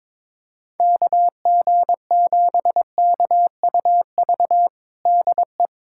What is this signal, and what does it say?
Morse code, heard as a keyed tone, about 22 wpm, 700 Hz. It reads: KG7KUV DE
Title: USA Callsign DE Callsign - 6 Characters - 22wpm